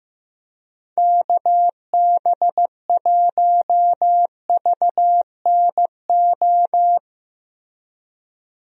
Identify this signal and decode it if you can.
Morse code, heard as a keyed tone, about 15 wpm, 700 Hz. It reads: KB1VNO
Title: USA Call Signs - 6 Characters - Repeated 3x - 15wpm